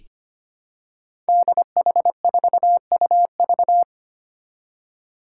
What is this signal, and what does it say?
Morse code, heard as a keyed tone, about 25 wpm, 700 Hz. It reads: DH4UV
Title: Five Challenging Letter-Number Combinations 25wpm